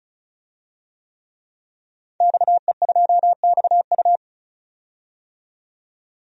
Morse code, heard as a keyed tone, about 34 wpm, 700 Hz. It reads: XE2XU
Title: DXCC-100 Call Signs - 5 Characters - 35wpm